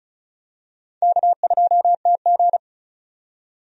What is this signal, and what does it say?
Morse code, heard as a keyed tone, about 34 wpm, 700 Hz. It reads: K2TG
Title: USA Call Signs - 4 Characters - 35wpm